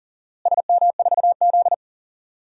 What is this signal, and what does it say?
Morse code, heard as a keyed tone, about 40 wpm, 700 Hz. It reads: SM4Z
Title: DXCC-100 Call Signs - 4 Characters - Repeated 2x - 40wpm